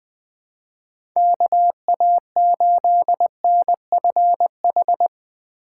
Morse code, heard as a keyed tone, about 20 wpm, 700 Hz. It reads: KA8NFH
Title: USA Call Signs - 6x Character Spacing - 20wpm